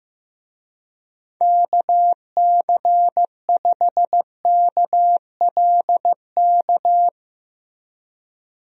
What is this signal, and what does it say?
Morse code, heard as a keyed tone, about 15 wpm, 700 Hz. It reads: KC5KLK
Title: USA Call Signs - 6 Characters - Repeated 2x - 15wpm